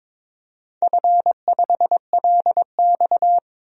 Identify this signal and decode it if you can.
Morse code, heard as a keyed tone, about 22 wpm, 700 Hz. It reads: F5LX
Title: DXCC-100 Call Signs - 4 Characters - 22wpm